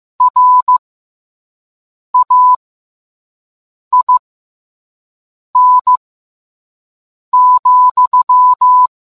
Morse code, heard as a keyed tone, about 15 wpm, spaced as for 5 wpm, 1000 Hz. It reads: RAIN,